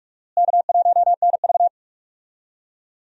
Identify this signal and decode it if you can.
Morse code, heard as a keyed tone, about 42 wpm, 700 Hz. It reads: K1NV
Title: USA Call Signs - 4 Characters - Repeated 3x - 45wpm